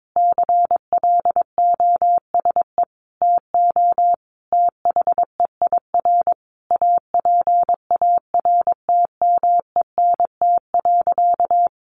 Morse code, heard as a keyed tone, about 22 wpm, 700 Hz. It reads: CLOSE TO THEIR APARTMENT.
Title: Sentences from Top 900 Words 22wpm